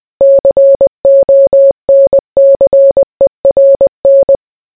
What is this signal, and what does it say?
Morse code, heard as a keyed tone, about 20 wpm, 550 Hz. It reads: CONCERN